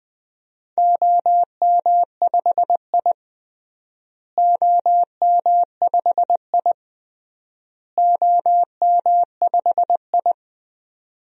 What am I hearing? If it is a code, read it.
Morse code, heard as a keyed tone, about 20 wpm, 700 Hz. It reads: OM5I OM5I OM5I
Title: DXCC-100 Call Signs - 4 Characters - Repeated 3x - 20wpm